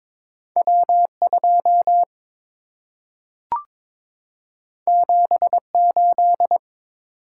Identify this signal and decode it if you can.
Morse code, heard as a keyed tone, about 22 wpm, 700 Hz. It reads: W2 78
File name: practice / 22wpm / Two Letter-Number Combinations - 22wpm